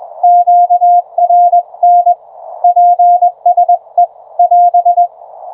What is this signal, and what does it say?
Morse code, heard as a keyed tone, about 21 wpm, 720 Hz. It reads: QRN PSE <AS>